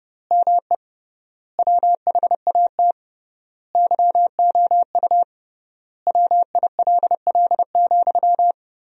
Morse code, heard as a keyed tone, about 30 wpm, 700 Hz. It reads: ME WHAT YOU WILL,
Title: 3x Word-Spacing - Sentences from Top 200 Words - 30wpm